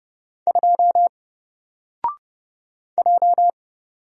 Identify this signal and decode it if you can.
Morse code, heard as a keyed tone, about 30 wpm, 700 Hz. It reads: 2 J
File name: Single-Letter-Number-30wpm